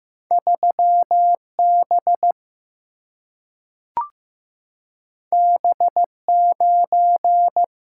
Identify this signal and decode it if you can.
Morse code, heard as a keyed tone, about 15 wpm, 700 Hz. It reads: 3B B9
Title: Single-Letter Single-Number 15wpm